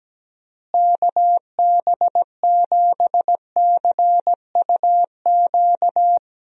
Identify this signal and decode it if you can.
Morse code, heard as a keyed tone, about 17 wpm, 700 Hz. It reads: KB7CUQ
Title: USA Call Signs - 6 Characters - Repeated 3x - 17wpm